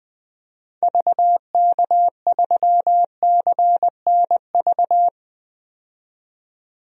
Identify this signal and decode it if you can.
Morse code, heard as a keyed tone, about 21 wpm, 700 Hz. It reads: VK3CNV
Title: DXCC-100 Call Signs - 6 Characters - 20wpm